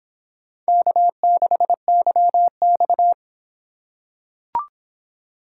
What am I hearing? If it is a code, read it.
Morse code, heard as a keyed tone, about 26 wpm, 700 Hz. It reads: K6YX